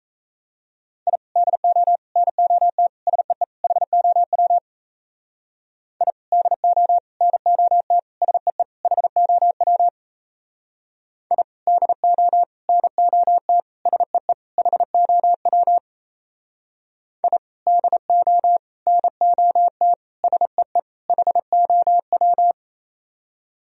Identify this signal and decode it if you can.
Morse code, heard as a keyed tone, about 37 wpm, 700 Hz. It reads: I DO NOT SEE HOW I DO NOT SEE HOW I DO NOT SEE HOW I DO NOT SEE HOW